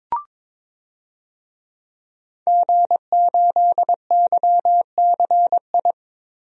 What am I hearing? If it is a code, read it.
Morse code, heard as a keyed tone, about 22 wpm, 700 Hz. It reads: G8YCI
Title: DXCC-100 Call Signs - 5 Characters - Repeated 2x - 22wpm